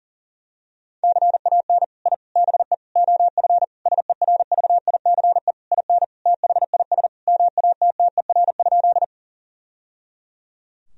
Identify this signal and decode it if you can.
Morse code, heard as a keyed tone, about 40 wpm, 700 Hz. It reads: CAN I BE OF SERVICE IN THIS MATTER?
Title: Sentences from Top 1000 Words - Review Part 4 - 40wpm